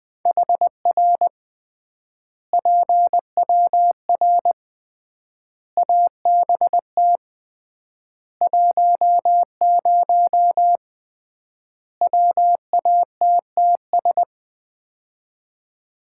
Morse code, heard as a keyed tone, about 19 wpm, 700 Hz. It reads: HR PWR ABT 10 WATTS
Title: Common QSO Phrases - 3x Word-Spacing - 20wpm